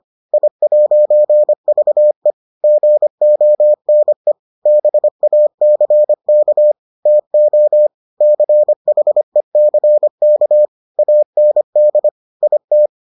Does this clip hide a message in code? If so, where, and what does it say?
Yes: Morse code from 0.3 to 12.9 s, heard as a keyed tone, about 25 wpm, 600 Hz. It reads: I'VE GONE BACK TO CHECK AND IT